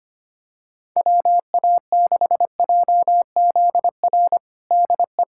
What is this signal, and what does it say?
Morse code, heard as a keyed tone, about 25 wpm, 700 Hz. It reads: WA6JZR DE
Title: USA Callsign DE Callsign - 6 Characters - 25wpm